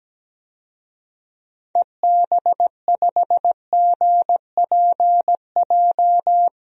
Morse code, heard as a keyed tone, about 17 wpm, 700 Hz. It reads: EB5GPJ
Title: DXCC-100 Call Signs - 6 Characters - Repeated 2x - 17wpm